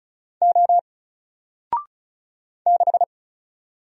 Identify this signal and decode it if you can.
Morse code, heard as a keyed tone, about 36 wpm, 700 Hz. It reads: O 6
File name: Single-Letter-Number-35wpm